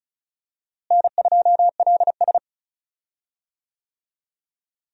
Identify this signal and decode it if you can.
Morse code, heard as a keyed tone, about 34 wpm, 700 Hz. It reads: N2LS
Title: USA Call Signs - 4 Characters - Repeated 2x - 35wpm